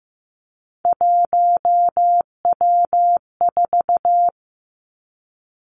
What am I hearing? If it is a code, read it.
Morse code, heard as a keyed tone, about 15 wpm, 700 Hz. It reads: 1W4
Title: Three Challenging Letter-Number Combinations 15wpm